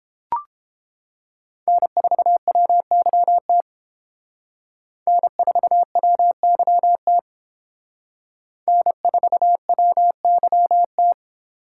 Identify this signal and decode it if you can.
Morse code, heard as a keyed tone, about 28 wpm, 700 Hz. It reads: N4WYT N4WYT N4WYT